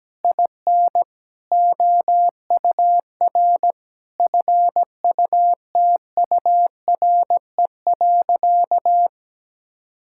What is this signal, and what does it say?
Morse code, heard as a keyed tone, about 17 wpm, 700 Hz. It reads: IN OUR FUTURE.
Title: Sentences from Top 1000 Words - Review Part 5 - 17wpm